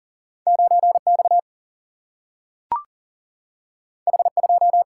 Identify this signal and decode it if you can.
Morse code, heard as a keyed tone, about 40 wpm, 700 Hz. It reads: 9 X H 2
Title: Single-Letter Single-Number 40wpm